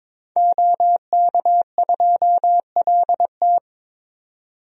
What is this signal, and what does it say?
Morse code, heard as a keyed tone, about 21 wpm, 700 Hz. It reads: OK2LT
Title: DXCC-100 Call Signs - 5 Characters - 22wpm